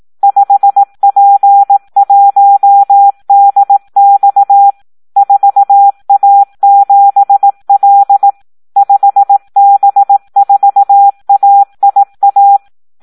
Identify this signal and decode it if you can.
Morse code, heard as a keyed tone, about 18 wpm, 800 Hz. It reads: 5P1DX 4A7L 5B4AIA
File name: Audio practica CW / SD 1 18